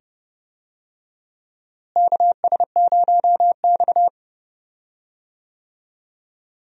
Morse code, heard as a keyed tone, about 30 wpm, 700 Hz. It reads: KS0X
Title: USA Call Signs - 4 Characters - Repeated 2x - 30wpm